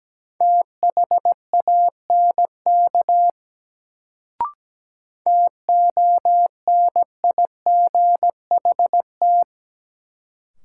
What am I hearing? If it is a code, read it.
Morse code, heard as a keyed tone, about 17 wpm, 700 Hz. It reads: THANK TONIGHT
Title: Sentences from Top 600 Words - New Word Drill - 17wpm